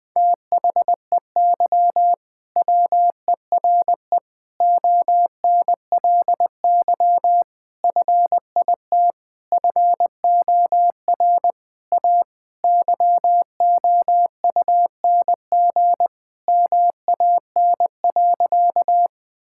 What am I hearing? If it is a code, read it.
Morse code, heard as a keyed tone, about 20 wpm, 700 Hz. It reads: THEY WERE ONLY FIT FOR A YOUNG MAN.
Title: Sentences from Top 1000 Words - Review Part 5 - 20wpm